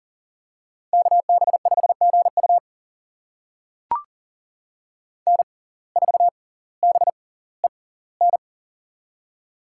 Morse code, heard as a keyed tone, about 40 wpm, spaced as for 20 wpm, 700 Hz. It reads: KB5GU N4BEN
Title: USA Call Signs - 6x Character Spacing - 40wpm